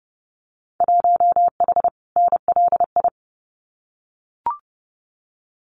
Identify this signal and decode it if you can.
Morse code, heard as a keyed tone, about 30 wpm, 700 Hz. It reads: 1H NLI
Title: Winter Field Day Exchanges 30wpm